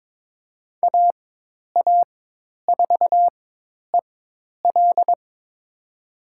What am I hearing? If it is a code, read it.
Morse code, heard as a keyed tone, about 23 wpm, spaced as for 10 wpm, 700 Hz. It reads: AA4EL
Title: USA Call Signs - 5 Characters - 4x Character Spacing - 22wpm